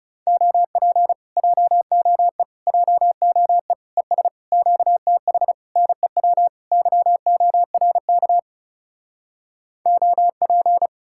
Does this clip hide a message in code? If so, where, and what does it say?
Yes: Morse code from 0.3 to 10.9 s, heard as a keyed tone, about 40 wpm, 700 Hz. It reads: OP JOE JOE ES QTH NEW YORK OP